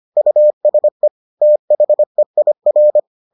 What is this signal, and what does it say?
Morse code, heard as a keyed tone, about 25 wpm, 600 Hz. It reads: USE THEIR